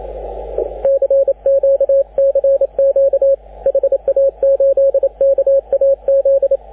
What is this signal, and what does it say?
Morse code, heard as a keyed tone, about 28 wpm, 550 Hz, with radio noise. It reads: CQ CQ HA8KAZ
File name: AUDIO2019 / HG6N CW SSB MIX